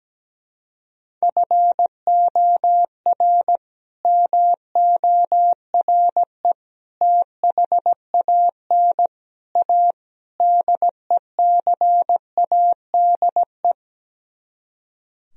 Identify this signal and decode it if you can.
Morse code, heard as a keyed tone, about 17 wpm, 700 Hz. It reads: FOR MORE THAN A DECADE
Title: Speed-Racing - Sets of 5 Words 17wpm